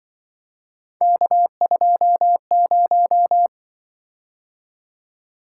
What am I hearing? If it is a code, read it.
Morse code, heard as a keyed tone, about 23 wpm, 700 Hz. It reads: K20
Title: Speed-Racing - Three Letter-Number Combinations - 20wpm